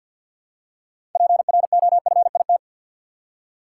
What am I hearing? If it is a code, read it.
Morse code, heard as a keyed tone, about 50 wpm, 700 Hz. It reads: PROFIT
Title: Six Letter Words - 50wpm